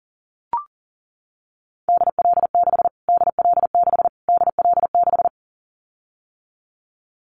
Morse code, heard as a keyed tone, about 40 wpm, 700 Hz. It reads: DL6 DL6 DL6
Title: Callsign Prefixes for Top 25 DXCC - Repeated 3x - 40wpm